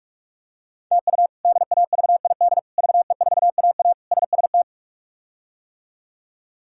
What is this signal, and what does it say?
Morse code, heard as a keyed tone, about 42 wpm, 700 Hz. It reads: TU DAVID VE4AA SST
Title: K1USN SST QSO Practice - 45wpm